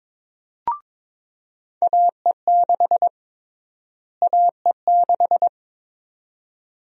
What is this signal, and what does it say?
Morse code, heard as a keyed tone, about 22 wpm, 700 Hz. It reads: AE6 AE6
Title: USA Call Sign Prefixes - Repeated 2x - 22wpm